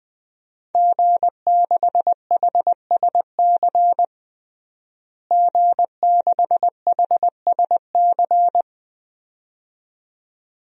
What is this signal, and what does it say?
Morse code, heard as a keyed tone, about 20 wpm, 700 Hz. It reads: G6HSC G6HSC
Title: DXCC-100 Call Signs - 5 Characters - Repeated 2x - 20wpm